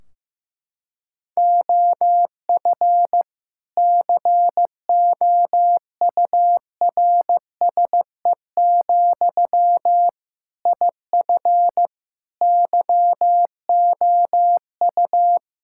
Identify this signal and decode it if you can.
Morse code, heard as a keyed tone, about 16 wpm, 700 Hz. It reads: OF COURSE, IF YOU
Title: Sentences from Top 1600 Words 15wpm